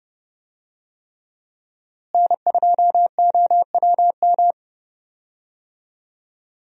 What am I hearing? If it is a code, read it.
Morse code, heard as a keyed tone, about 30 wpm, 700 Hz. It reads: N2OWM
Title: USA Call Signs - 5 Characters - Repeated 3x - 30wpm